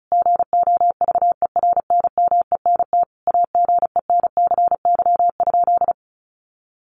Morse code, heard as a keyed tone, about 34 wpm, 700 Hz. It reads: GOVERNMENT AGENCY?
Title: Sentences from Top 1300 Words 35wpm